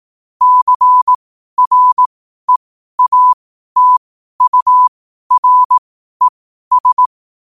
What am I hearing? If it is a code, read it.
Morse code, heard as a keyed tone, about 18 wpm, spaced as for 12 wpm, 1000 Hz. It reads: CREATURES
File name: qotdfiles / 12wpm